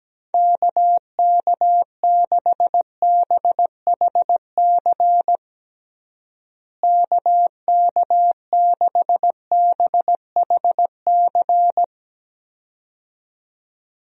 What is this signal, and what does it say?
Morse code, heard as a keyed tone, about 17 wpm, 700 Hz. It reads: KK6BHC KK6BHC
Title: USA Call Signs - 6 Characters - Repeated 2x - 17wpm